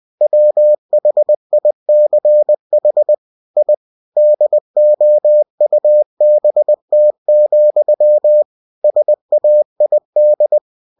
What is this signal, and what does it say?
Morse code, heard as a keyed tone, about 20 wpm, 600 Hz. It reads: WHICH I DOUBT, SAID
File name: cw-books / winnie-the-pooh / winnie-20wpm-page-0045